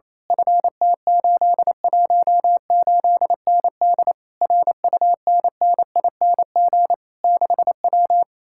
Morse code, heard as a keyed tone, about 27 wpm, 700 Hz. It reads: FT818ND RUNNING 6W
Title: Speed-Racing Common QSO Phrases - 28wpm